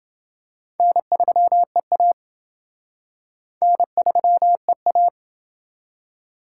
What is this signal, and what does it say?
Morse code, heard as a keyed tone, about 27 wpm, 700 Hz. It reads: N3EA N3EA